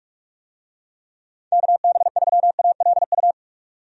Morse code, heard as a keyed tone, about 44 wpm, 700 Hz. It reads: KB3ALU
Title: USA Call Signs - 6 Characters - 45wpm